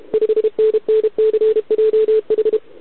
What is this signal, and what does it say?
Morse code, heard as a keyed tone, about 28 wpm, 440 Hz. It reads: 5NNCJH
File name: ARCHIV / OKDX2020 / OK1RIb